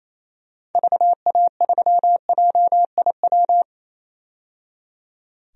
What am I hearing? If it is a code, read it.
Morse code, heard as a keyed tone, about 28 wpm, 700 Hz. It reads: VA3JIW
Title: DXCC-100 Call Signs - 6 Characters - 28wpm